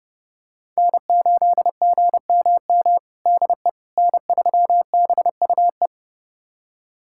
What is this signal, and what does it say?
Morse code, heard as a keyed tone, about 30 wpm, 700 Hz. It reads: N8GMM DE N3BUE
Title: USA Callsign DE Callsign - 5 Characters - 30wpm